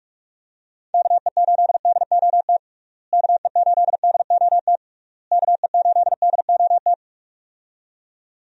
Morse code, heard as a keyed tone, about 46 wpm, 700 Hz. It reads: KE8DOT KE8DOT KE8DOT
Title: USA Call Signs - 6 Characters - Repeated 3x - 45wpm